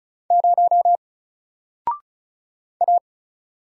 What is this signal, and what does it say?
Morse code, heard as a keyed tone, about 34 wpm, 700 Hz. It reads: 0 A
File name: Single-Letter-Number-35wpm